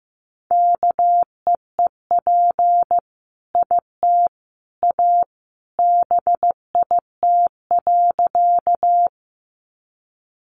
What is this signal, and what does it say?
Morse code, heard as a keyed tone, about 15 wpm, 700 Hz. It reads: KEEP IT A BIT.
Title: Sentences from Top 900 Words 15wpm